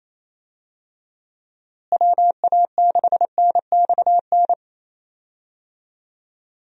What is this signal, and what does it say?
Morse code, heard as a keyed tone, about 28 wpm, 700 Hz. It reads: WA6NXN
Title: USA Call Signs - 6 Characters - Repeated 2x - 28wpm